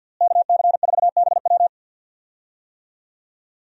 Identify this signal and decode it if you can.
Morse code, heard as a keyed tone, about 52 wpm, 700 Hz. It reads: KC4BW
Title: USA Call Signs - 5 Characters - Repeated 3x - 50wpm